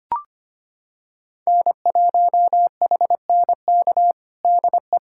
Morse code, heard as a keyed tone, about 25 wpm, 700 Hz. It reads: N1HNK DE
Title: USA Callsign DE Callsign - 5 Characters - 25wpm